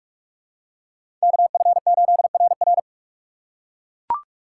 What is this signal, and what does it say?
Morse code, heard as a keyed tone, about 44 wpm, 700 Hz. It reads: KF8RR